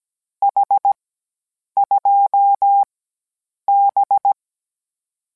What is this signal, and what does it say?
Morse code, heard as a keyed tone, about 17 wpm, 800 Hz. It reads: H 2 B